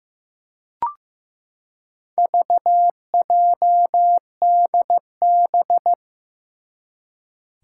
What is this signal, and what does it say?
Morse code, heard as a keyed tone, about 16 wpm, 700 Hz. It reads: VJDB
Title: Four Challenging Letter-Number Combinations 15wpm